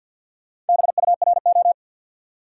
Morse code, heard as a keyed tone, about 51 wpm, 700 Hz. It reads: BURO
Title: Top 90 QSO Elements 50wpm